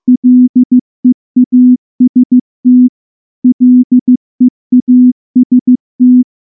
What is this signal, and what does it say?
Morse code, heard as a keyed tone, about 14 wpm, 250 Hz. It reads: LEAST LEAST
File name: morse / morse-training-15WPM